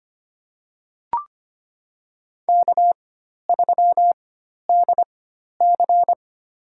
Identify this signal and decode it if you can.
Morse code, heard as a keyed tone, about 25 wpm, spaced as for 12 wpm, 700 Hz. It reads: K3DC
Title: USA Call Signs - 4 Characters - 4x Character Spacing - 25wpm